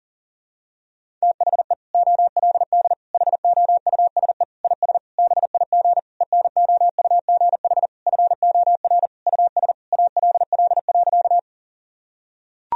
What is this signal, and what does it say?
Morse code, heard as a keyed tone, about 40 wpm, 700 Hz. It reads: THE OLD HOUSE IS BIG ENOUGH FOR US ALL.
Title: Sentences from Top 1000 Words - Review Part 1 - 40wpm